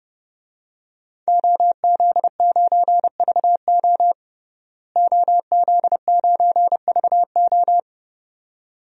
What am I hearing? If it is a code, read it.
Morse code, heard as a keyed tone, about 29 wpm, 700 Hz. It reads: OZ9VO OZ9VO
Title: DXCC-100 Call Signs - 5 Characters - Repeated 2x - 30wpm